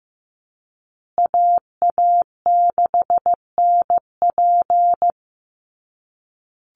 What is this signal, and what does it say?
Morse code, heard as a keyed tone, about 15 wpm, 700 Hz. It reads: AA6NP
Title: USA Call Signs Repeated 3x 15wpm